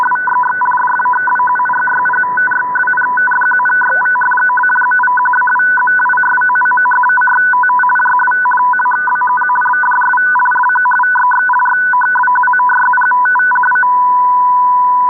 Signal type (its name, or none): two-tone teleprinter signal